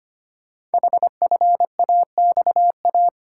Morse code, heard as a keyed tone, about 25 wpm, 700 Hz. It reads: HFAXA
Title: Code Groups - 5 Letters - 25wpm